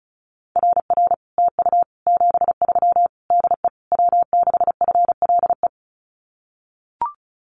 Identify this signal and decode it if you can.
Morse code, heard as a keyed tone, about 36 wpm, 700 Hz. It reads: RR TU 73 DE W6FLE
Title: Common QSO Phrases Encore 35wpm